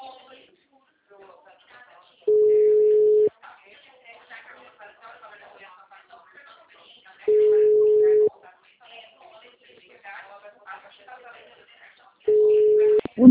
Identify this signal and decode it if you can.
Ringback tone (Europe, 425 Hz), with speech.
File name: out- -2032- - - .96350